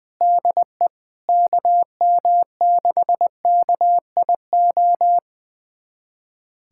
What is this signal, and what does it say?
Morse code, heard as a keyed tone, about 20 wpm, 700 Hz. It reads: DE KM6KIO
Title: USA Callsign DE Callsign 20wpm